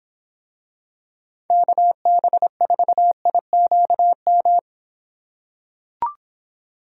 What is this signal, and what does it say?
Morse code, heard as a keyed tone, about 26 wpm, 700 Hz. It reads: KB4IQM